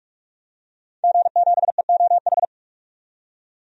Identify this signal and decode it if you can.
Morse code, heard as a keyed tone, about 46 wpm, 700 Hz. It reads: G7EOH
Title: DXCC-100 Call Signs - 5 Characters - 45wpm